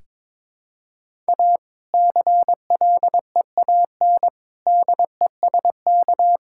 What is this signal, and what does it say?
Morse code, heard as a keyed tone, about 22 wpm, 700 Hz. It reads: A CLEAN DESK
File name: Difficult Sentences with Hints - 22wpm